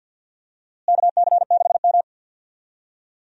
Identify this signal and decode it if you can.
Morse code, heard as a keyed tone, about 51 wpm, 700 Hz. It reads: KC6M
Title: USA Call Signs - 4 Characters - Repeated 3x - 50wpm